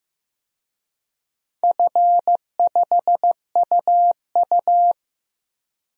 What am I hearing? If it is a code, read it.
Morse code, heard as a keyed tone, about 15 wpm, 700 Hz. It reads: F5UU
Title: DXCC-100 Call Signs - 4 Characters - 15wpm